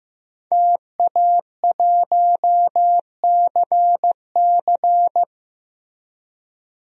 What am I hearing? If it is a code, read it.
Morse code, heard as a keyed tone, about 15 wpm, 700 Hz. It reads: TA1CC
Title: DXCC-100 Call Signs - 5 Characters - 15wpm